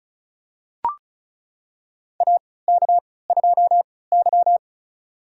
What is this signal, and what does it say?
Morse code, heard as a keyed tone, about 34 wpm, spaced as for 20 wpm, 700 Hz. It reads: AK2Y
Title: USA Call Signs - 4 Characters - 3x Character Spacing - 35wpm